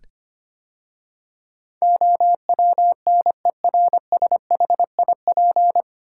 Morse code, heard as a keyed tone, about 25 wpm, 700 Hz. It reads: OWNERSHIP
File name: practice / 25wpm / 9 Letter Words - First 3 Letters Given - 25wpm